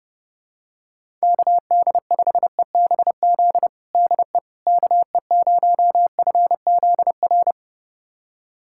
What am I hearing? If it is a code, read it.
Morse code, heard as a keyed tone, about 32 wpm, 700 Hz. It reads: KD5EBZ DE KE0FZR
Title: USA Callsign DE Callsign - 6 Characters - 30wpm